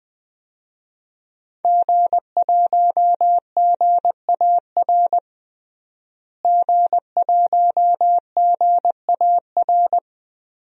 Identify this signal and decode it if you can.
Morse code, heard as a keyed tone, about 20 wpm, 700 Hz. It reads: G1GAR G1GAR
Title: DXCC-100 Call Signs - 5 Characters - Repeated 2x - 20wpm